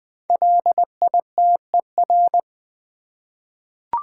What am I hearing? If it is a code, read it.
Morse code, heard as a keyed tone, about 20 wpm, 700 Hz. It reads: LITER
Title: 5 Letter Words - First 3 Letters Given - 20wpm